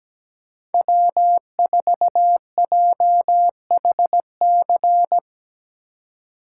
Morse code, heard as a keyed tone, about 17 wpm, 700 Hz. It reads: W4JHC
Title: USA Call Signs - 5 Characters - 3x Character Spacing - 17wpm